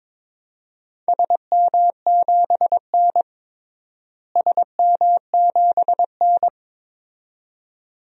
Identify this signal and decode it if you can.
Morse code, heard as a keyed tone, about 22 wpm, 700 Hz. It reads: SM7N SM7N
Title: DXCC-100 Call Signs - 4 Characters - Repeated 2x - 22wpm